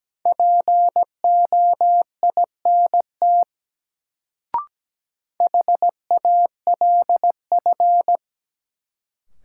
Morse code, heard as a keyed tone, about 17 wpm, 700 Hz. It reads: POINT HALF
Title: Sets of 3 Words - New Word Drill - 17wpm